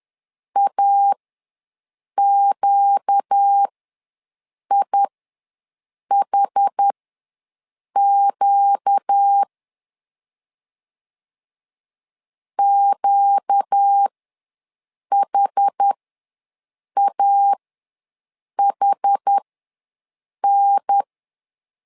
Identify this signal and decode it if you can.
Morse code, heard as a keyed tone, about 10 wpm, spaced as for 6 wpm, 800 Hz. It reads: AQIHQ QHAHN
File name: LEZIONI / 4 Q ws 10wpm 2306